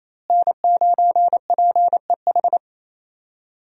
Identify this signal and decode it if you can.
Morse code, heard as a keyed tone, about 28 wpm, 700 Hz. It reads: N9PEH
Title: USA Call Signs - 5 Characters - Repeated 2x - 28wpm